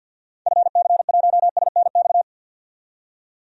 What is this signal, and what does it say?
Morse code, heard as a keyed tone, about 47 wpm, 700 Hz. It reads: FC1SNX
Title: DXCC-100 Call Signs - 6 Characters - 50wpm